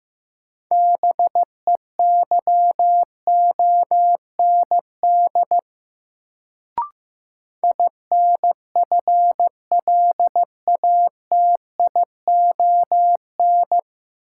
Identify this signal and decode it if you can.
Morse code, heard as a keyed tone, about 15 wpm, 700 Hz. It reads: BEYOND INFLATION
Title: Sentences from Top 1100 Words - New Word Drill Alt - 15wpm